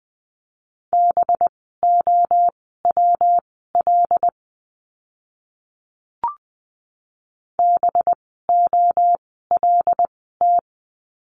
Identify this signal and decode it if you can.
Morse code, heard as a keyed tone, about 20 wpm, spaced as for 15 wpm, 700 Hz. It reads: BOWL BOLT
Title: 185 Lesson 30f - Letter b - Words 20wpm